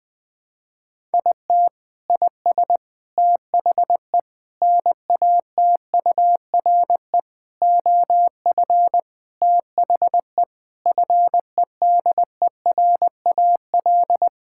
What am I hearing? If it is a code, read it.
Morse code, heard as a keyed tone, about 20 wpm, 700 Hz. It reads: IT IS THE NATURE OF THE FEDERAL
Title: Sentences from Top 1000 Words - Review Part 5 - 20wpm